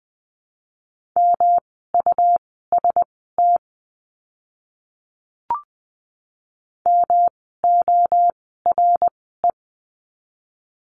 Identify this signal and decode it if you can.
Morse code, heard as a keyed tone, about 20 wpm, spaced as for 14 wpm, 700 Hz. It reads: MUST MORE
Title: 237 Lesson 38f - Letter q - Words 20wpm